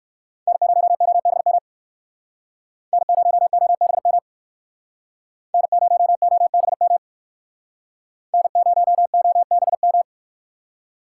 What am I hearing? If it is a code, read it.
Morse code, heard as a keyed tone, about 59 wpm, 700 Hz. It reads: N0OBM N0OBM N0OBM N0OBM